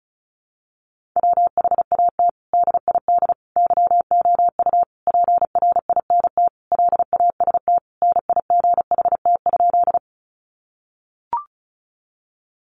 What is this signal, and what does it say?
Morse code, heard as a keyed tone, about 34 wpm, 700 Hz. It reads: WHAT DID YOU PRINT LAST NIGHT?
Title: Sentences from Top 1400 Words 35wpm